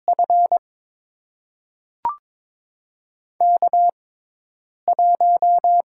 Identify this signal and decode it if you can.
Morse code, heard as a keyed tone, about 22 wpm, spaced as for 8 wpm, 700 Hz. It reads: F K1
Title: USA Call Signs - 5 Characters - 6x Character Spacing - 22wpm